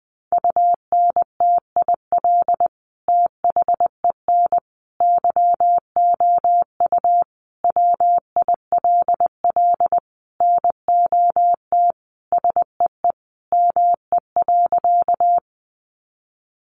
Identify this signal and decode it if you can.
Morse code, heard as a keyed tone, about 20 wpm, 700 Hz. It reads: UNTIL THEN YOU WILL NOT SEE ME.
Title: Sentences from Top 600 Words 20wpm